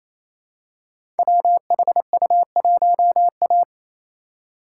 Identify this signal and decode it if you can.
Morse code, heard as a keyed tone, about 28 wpm, 700 Hz. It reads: WHU1A
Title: Code Groups - 5 Letter:Numbers - 28wpm